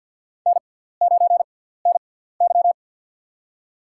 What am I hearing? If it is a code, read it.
Morse code, heard as a keyed tone, about 51 wpm, spaced as for 17 wpm, 700 Hz. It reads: N9NY
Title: USA Call Signs - 4 Characters - 6x Character Spacing - 50wpm